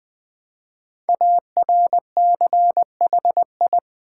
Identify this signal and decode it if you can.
Morse code, heard as a keyed tone, about 20 wpm, 700 Hz. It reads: ARCHI
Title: Build A Word - 5 Letters At A Time - 12-Letter Words - 20wpm